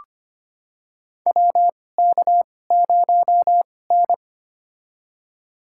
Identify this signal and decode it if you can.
Morse code, heard as a keyed tone, about 25 wpm, spaced as for 18 wpm, 700 Hz. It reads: WK0N
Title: USA Call Signs - 2x Character Spacing - 25wpm